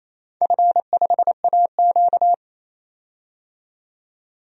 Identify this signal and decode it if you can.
Morse code, heard as a keyed tone, about 28 wpm, 700 Hz. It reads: F5AQ
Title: DXCC-100 Call Signs - 4 Characters - 28wpm